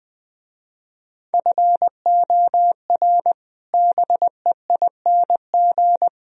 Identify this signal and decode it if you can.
Morse code, heard as a keyed tone, about 21 wpm, 700 Hz. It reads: FOR BEING
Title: Speed-Racing - Sets of 4 Words 20wpm